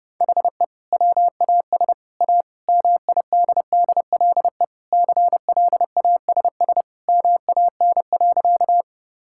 Morse code, heard as a keyed tone, about 30 wpm, 700 Hz. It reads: HE WAS A MIDDLE CLASS MAN.
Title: Sentences from Top 1000 Words - Review Part 6 - 30wpm